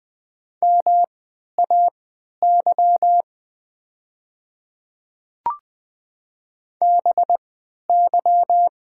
Morse code, heard as a keyed tone, about 20 wpm, spaced as for 11 wpm, 700 Hz. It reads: MAY BY
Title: 3x Character Spacing - Top 100 Words - 20wpm